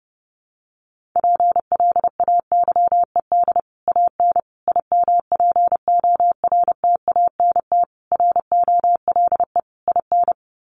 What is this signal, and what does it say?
Morse code, heard as a keyed tone, about 30 wpm, 700 Hz. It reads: PLAYED AN IMPORTANT ROLE IN
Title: Sets of 5 Words Encore2 30wpm